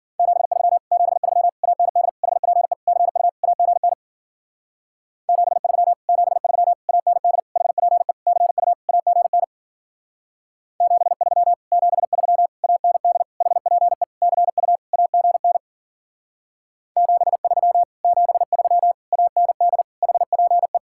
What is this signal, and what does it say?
Morse code, heard as a keyed tone, about 50 wpm, 700 Hz. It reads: 73 73 AND HPE CU AGN 73 73 AND HPE CU AGN 73 73 AND HPE CU AGN 73 73 AND HPE
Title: Speed-Racing Common QSO Phrases Encore - 40wpm